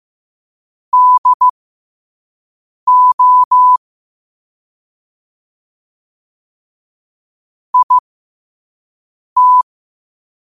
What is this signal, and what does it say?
Morse code, heard as a keyed tone, about 15 wpm, spaced as for 5 wpm, 1000 Hz. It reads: DO IT
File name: qotdfiles / 5wpm